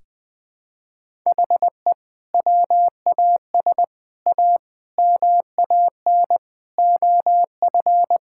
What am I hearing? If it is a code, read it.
Morse code, heard as a keyed tone, about 20 wpm, 700 Hz. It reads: HE WAS A MAN OF
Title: Sentences from Top 1000 Words - Review Part 6 - 20wpm